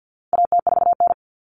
Morse code, heard as a keyed tone, about 50 wpm, 700 Hz. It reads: AT4N
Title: Common Cut Numbers - 4-Digit 50wpm